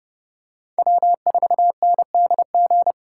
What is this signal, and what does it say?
Morse code, heard as a keyed tone, about 28 wpm, 700 Hz. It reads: W4NDG